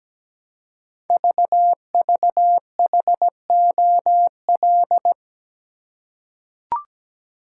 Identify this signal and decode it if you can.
Morse code, heard as a keyed tone, about 17 wpm, 700 Hz. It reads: VVHOL
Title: Code Groups - 5 Letters - 17wpm